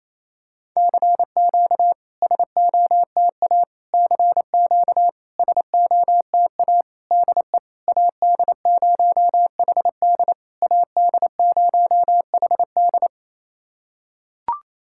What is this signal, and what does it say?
Morse code, heard as a keyed tone, about 28 wpm, 700 Hz. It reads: CQ SOTA CQ SOTA DE AD0HD AD0HD
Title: SOTA QSO Practice 28wpm